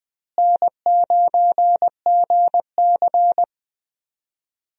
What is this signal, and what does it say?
Morse code, heard as a keyed tone, about 20 wpm, 700 Hz. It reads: N9GC
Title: USA Call Signs - 4 Characters - Repeated 3x - 20wpm